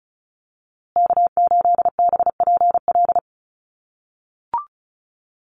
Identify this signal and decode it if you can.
Morse code, heard as a keyed tone, about 34 wpm, 700 Hz. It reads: K8BPL
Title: USA Call Signs Repeated 2x 35wpm